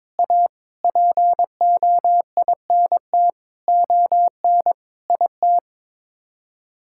Morse code, heard as a keyed tone, about 21 wpm, 700 Hz. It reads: A POINT ON IT
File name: Sets of 9 Words - 22wpm